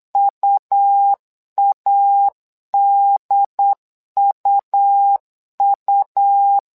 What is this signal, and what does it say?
Morse code, heard as a keyed tone, about 8 wpm, 800 Hz. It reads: UADUU
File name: LEZIONI / 2 UHDN / 2 UHD 8wpm 2258